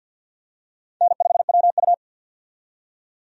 Morse code, heard as a keyed tone, about 52 wpm, 700 Hz. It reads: N5WU
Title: USA Call Signs - 4 Characters - 50wpm